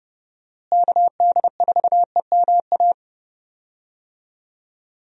Morse code, heard as a keyed tone, about 31 wpm, 700 Hz. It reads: KD4EMA
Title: USA Call Signs - 6 Characters - Repeated 2x - 30wpm